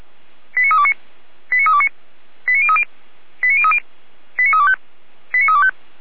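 Five-tone selective call (ZVEI), heard as quick runs of five tones, tones of about 70 ms. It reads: 89329 89329 89030 89030 89325 89325
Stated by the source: ZVEI five-tone selective call received on SDR.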